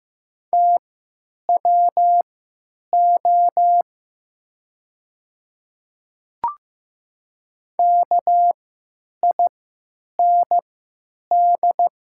Morse code, heard as a keyed tone, about 15 wpm, spaced as for 9 wpm, 700 Hz. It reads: TWO KIND
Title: 3x Character Spacing - Top 300 Words - 15wpm